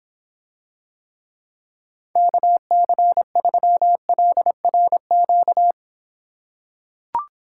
Morse code, heard as a keyed tone, about 26 wpm, 700 Hz. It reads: KC3LRQ